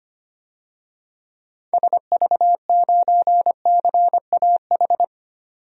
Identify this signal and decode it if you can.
Morse code, heard as a keyed tone, about 25 wpm, 700 Hz. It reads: SV9CAH
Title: DXCC-100 Call Signs - 6 Characters - 25wpm